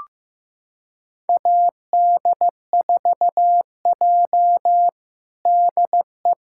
Morse code, heard as a keyed tone, about 15 wpm, 700 Hz. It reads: AD4J DE
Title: USA Callsign DE Callsign - 4 Characters - 15wpm